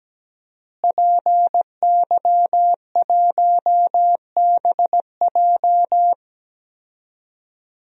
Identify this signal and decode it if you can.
Morse code, heard as a keyed tone, about 17 wpm, 700 Hz. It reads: PY1BJ
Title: DXCC-100 Call Signs - 5 Characters - Repeated 3x - 17wpm